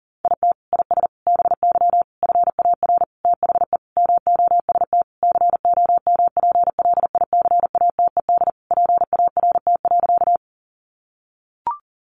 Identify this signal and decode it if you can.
Morse code, heard as a keyed tone, about 40 wpm, 700 Hz. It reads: IT IS BY FAR THE MOST COMPLICATED PART.
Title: Sentences from Top 1200 Words 40wpm